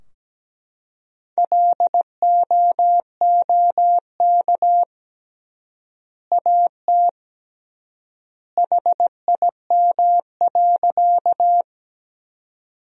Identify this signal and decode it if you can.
Morse code, heard as a keyed tone, about 17 wpm, 700 Hz. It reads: LOOK AT HIM.
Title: 3x Word-Spacing - Sentences from Top 100 Words - 17wpm